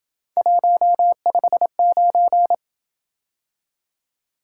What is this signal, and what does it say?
Morse code, heard as a keyed tone, about 27 wpm, 700 Hz. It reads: 159